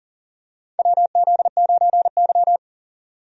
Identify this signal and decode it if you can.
Morse code, heard as a keyed tone, about 40 wpm, 700 Hz. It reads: WZ9Y
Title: USA Call Signs - 4 Characters - 40wpm